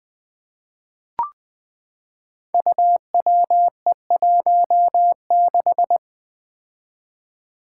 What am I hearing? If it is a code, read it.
Morse code, heard as a keyed tone, about 20 wpm, 700 Hz. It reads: UWE16
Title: Five Challenging Letter-Number Combinations 20wpm